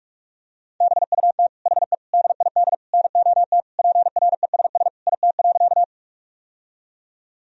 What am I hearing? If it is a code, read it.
Morse code, heard as a keyed tone, about 46 wpm, 700 Hz. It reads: BUT HE DID NOT PRESS IT.
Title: Sentences from Top 1000 Words - Review Part 1 - 45wpm